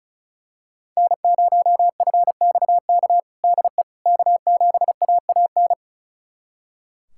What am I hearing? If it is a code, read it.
Morse code, heard as a keyed tone, about 36 wpm, 700 Hz. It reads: N0FXK DE K7AAN
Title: USA Callsign DE Callsign - 5 Characters - 35wpm